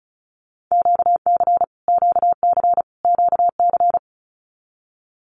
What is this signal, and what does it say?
Morse code, heard as a keyed tone, about 34 wpm, 700 Hz. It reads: QC QC QC
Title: Canadian Provinces and Territories - Repeated 3x - 35wpm